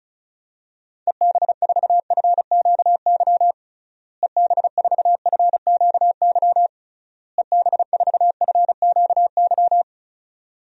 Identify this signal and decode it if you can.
Morse code, heard as a keyed tone, about 36 wpm, 700 Hz. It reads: EB4FQY EB4FQY EB4FQY
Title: DXCC-100 Call Signs - 6 Characters - Repeated 3x - 35wpm